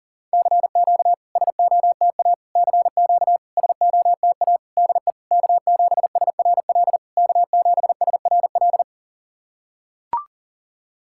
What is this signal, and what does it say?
Morse code, heard as a keyed tone, about 40 wpm, 690 Hz. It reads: CQ SOTA CQ SOTA DE K7SRL K7SRL
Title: SOTA QSO Practice 40wpm